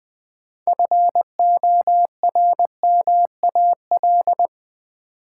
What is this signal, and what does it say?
Morse code, heard as a keyed tone, about 20 wpm, 700 Hz. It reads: FORMAL
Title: Sentences from Top 1300 Words - New Word Drill - 20wpm